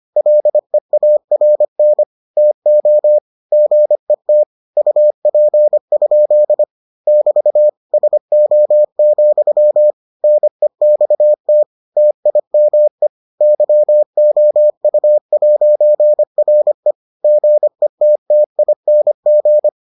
Morse code, heard as a keyed tone, about 25 wpm, 600 Hz. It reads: LEARN TO GET UP? <BT> SO, NEXT TIME YOU'RE GETTING